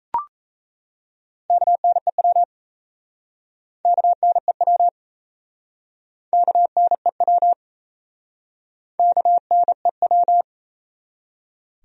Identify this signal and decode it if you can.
Morse code, heard as a keyed tone, about 37 wpm, 700 Hz. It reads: KNEW KNEW KNEW KNEW